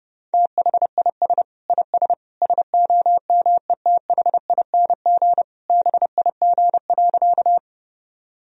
Morse code, heard as a keyed tone, about 32 wpm, 700 Hz. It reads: THIS IS SOMETHING BIG.
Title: Sentences from Top 1000 Words - Review Part 1 - 30wpm